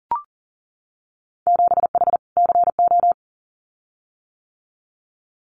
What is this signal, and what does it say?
Morse code, heard as a keyed tone, about 41 wpm, 700 Hz. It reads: 7H CO
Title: Winter Field Day Exchanges Encore 40wpm